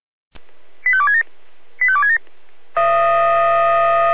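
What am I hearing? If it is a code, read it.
Five-tone selective call (ZVEI), heard as quick runs of five tones, tones of about 70 ms. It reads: 85267 85267
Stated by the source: ZVEI five-tone selective call received on an SDR.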